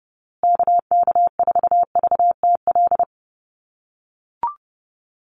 Morse code, heard as a keyed tone, about 30 wpm, 700 Hz. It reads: KK4VTL
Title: USA Call Signs Repeated 2x 30wpm